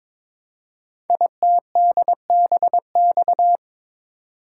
Morse code, heard as a keyed tone, about 22 wpm, 700 Hz. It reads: ITDBX
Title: Code Groups - 5 Letters - 22wpm